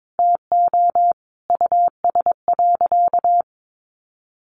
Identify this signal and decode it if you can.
Morse code, heard as a keyed tone, about 22 wpm, 700 Hz. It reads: TO US.
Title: Sentences from Top 1400 Words 22wpm